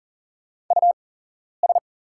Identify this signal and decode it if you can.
Morse code, heard as a keyed tone, about 40 wpm, spaced as for 11 wpm, 690 Hz. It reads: US
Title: 8x Character Spacing - Top 300 Words - 40wpm